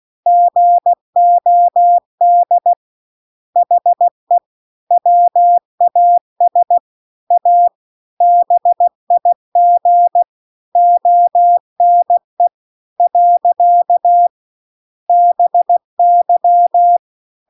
Morse code, heard as a keyed tone, about 16 wpm, 700 Hz. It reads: GOD HE WAS A BIG ONE . BY